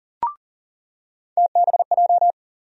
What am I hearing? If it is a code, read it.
Morse code, heard as a keyed tone, about 39 wpm, 700 Hz. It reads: TBJ
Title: USA Call Sign Postfixes - 3 Letters 40wpm